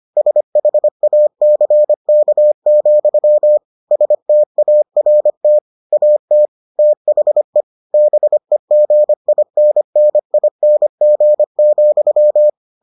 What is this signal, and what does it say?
Morse code, heard as a keyed tone, about 25 wpm, 600 Hz. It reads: SHACK, START AT THE BEGINNING,